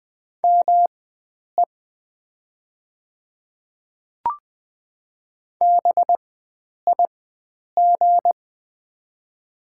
Morse code, heard as a keyed tone, about 20 wpm, spaced as for 9 wpm, 700 Hz. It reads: ME BIG
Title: 4x Character Spacing - Top 300 Words - 20wpm